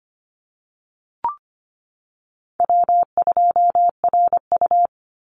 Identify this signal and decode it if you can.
Morse code, heard as a keyed tone, about 25 wpm, 700 Hz. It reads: W2RU
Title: Easy Call Signs - 25wpm